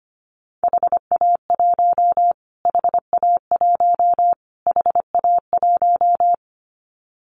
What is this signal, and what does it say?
Morse code, heard as a keyed tone, about 25 wpm, 700 Hz. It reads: HA1 HA1 HA1
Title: Callsign Prefixes for Top 25 DXCC - Repeated 3x - 25wpm